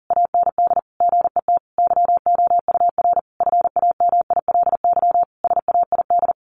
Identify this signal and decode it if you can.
Morse code, heard as a keyed tone, about 40 wpm, 700 Hz. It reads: AND GET YOUR FAMILY SAID
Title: Sentences from Top 1100 Words 40wpm